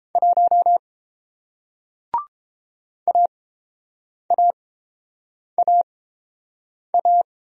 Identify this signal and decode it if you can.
Morse code, heard as a keyed tone, about 30 wpm, spaced as for 7 wpm, 700 Hz. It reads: 1 AAAA